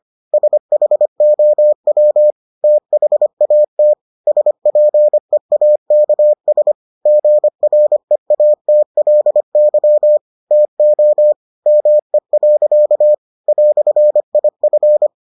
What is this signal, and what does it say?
Morse code, heard as a keyed tone, about 25 wpm, 600 Hz. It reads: SHOW THAT SPEAKS GREATLY TO ME. "IF